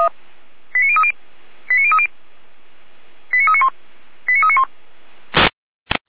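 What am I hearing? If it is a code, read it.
Five-tone selective call (ZVEI), heard as quick runs of five tones, tones of about 70 ms. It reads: 89030 89030 89391 89391
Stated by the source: ZVEI five-tone selective call received on SDR.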